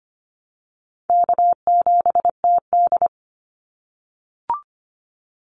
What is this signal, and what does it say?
Morse code, heard as a keyed tone, about 25 wpm, 700 Hz. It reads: K7TD
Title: Call Signs - 25wpm